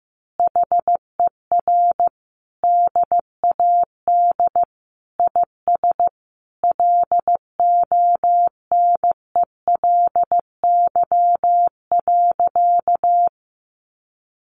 Morse code, heard as a keyed tone, about 15 wpm, 700 Hz. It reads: HER DAD IS LONELY.
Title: Additional Easy Phrases 15wpm